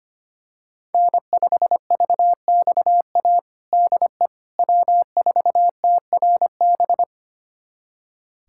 Morse code, heard as a keyed tone, about 25 wpm, 700 Hz. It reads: N5VXA DE W4TRB
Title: USA Callsign DE Callsign - 5 Characters - 25wpm